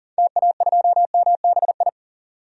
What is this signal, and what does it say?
Morse code, heard as a keyed tone, about 40 wpm, 700 Hz. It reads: TA2MBI
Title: DXCC-100 Call Signs - 6 Characters - 40wpm